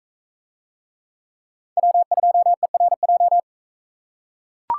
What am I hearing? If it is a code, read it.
Morse code, heard as a keyed tone, about 39 wpm, 700 Hz. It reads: W2ERJ